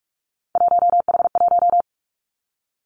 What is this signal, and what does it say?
Morse code, heard as a keyed tone, about 46 wpm, 700 Hz. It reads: 1H1
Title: Three Challenging Letter-Number Combinations 45wpm